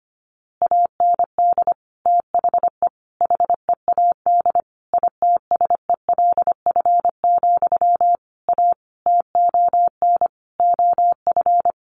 Morse code, heard as a keyed tone, about 25 wpm, 700 Hz. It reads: AND THE HEAD ITSELF, A TON OF
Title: A Sound Of Thunder 25wpm